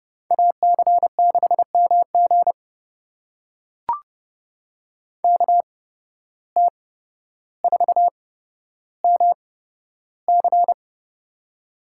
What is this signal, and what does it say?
Morse code, heard as a keyed tone, about 30 wpm, spaced as for 13 wpm, 700 Hz. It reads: AC6MG KT4MC
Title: USA Call Signs - 5 Characters - 8x Character Spacing - 30wpm